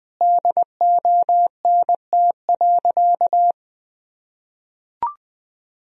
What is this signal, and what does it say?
Morse code, heard as a keyed tone, about 21 wpm, 700 Hz. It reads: DONT.
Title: Sentences from Top 1000 Words - Review Part 3 - 20wpm